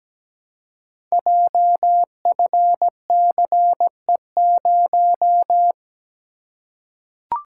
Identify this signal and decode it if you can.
Morse code, heard as a keyed tone, about 17 wpm, 700 Hz. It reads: JFCE0
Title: Code Groups - 5 Letter:Numbers - 17wpm